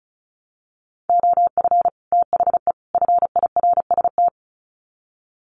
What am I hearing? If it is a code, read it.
Morse code, heard as a keyed tone, about 34 wpm, 700 Hz. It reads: OF THE FIRST
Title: Sets of 3 Words 35wpm